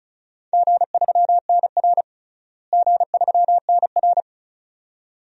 Morse code, heard as a keyed tone, about 36 wpm, 700 Hz. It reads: G3NR G3NR
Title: DXCC-100 Call Signs - 4 Characters - Repeated 2x - 35wpm